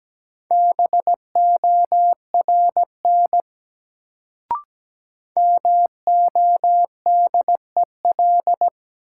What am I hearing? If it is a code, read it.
Morse code, heard as a keyed tone, about 17 wpm, 700 Hz. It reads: BORN MODEL
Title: Sentences from Top 1000 Words - New Word Drill - 17wpm